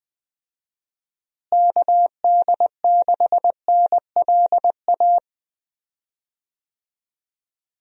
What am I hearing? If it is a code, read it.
Morse code, heard as a keyed tone, about 20 wpm, 700 Hz. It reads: KD6NLA
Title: USA Call Signs - 6 Characters - Repeated 2x - 20wpm